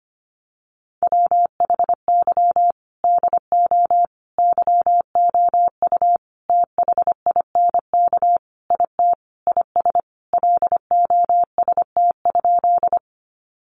Morse code, heard as a keyed tone, about 25 wpm, 700 Hz. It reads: WHY DO YOU THINK IT IS LOST?
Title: Sentences from Top 1400 Words 25wpm